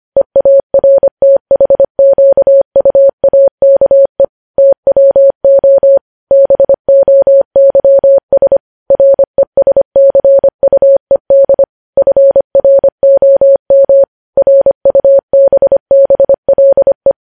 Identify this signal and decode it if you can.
Morse code, heard as a keyed tone, about 25 wpm, 550 Hz. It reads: EARTHQUAKE TWO BOYS RESCUED FROM RUBBLE
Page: news-headlines-in-morse-code-at-25-wpm